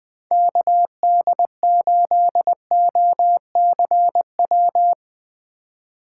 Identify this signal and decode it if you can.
Morse code, heard as a keyed tone, about 20 wpm, 700 Hz. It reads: KD8OCW
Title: 2023 POTA Hunter Call Signs - 20wpm